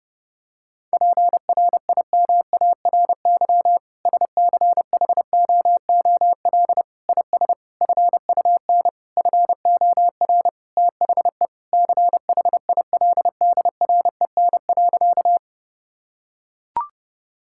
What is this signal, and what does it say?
Morse code, heard as a keyed tone, about 30 wpm, 700 Hz. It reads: PRIMARY SCHOOL IS FUN FOR THE CHILDREN.
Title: Sentences from Top 1000 Words - Review Part 5 - 30wpm